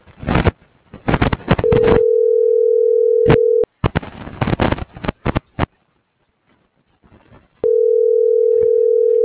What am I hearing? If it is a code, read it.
Ringback tone (North America), 440 + 480 Hz.